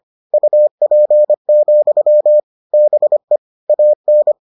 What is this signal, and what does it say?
Morse code, heard as a keyed tone, about 25 wpm, 600 Hz. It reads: UP, BE AN